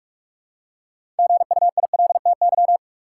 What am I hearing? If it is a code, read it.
Morse code, heard as a keyed tone, about 46 wpm, 700 Hz. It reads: GUILTY
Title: Six Letter Words - 45wpm